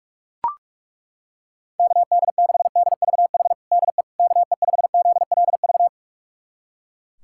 Morse code, heard as a keyed tone, about 41 wpm, 690 Hz. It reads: KD6DVH DE KE5ZLV
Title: USA Callsign DE Callsign 45wpm